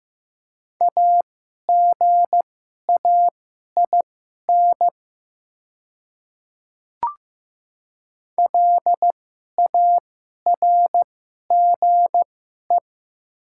Morse code, heard as a keyed tone, about 15 wpm, spaced as for 11 wpm, 700 Hz. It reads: AGAIN LARGE
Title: 2x Character Spacing - Top 200 Words - 15wpm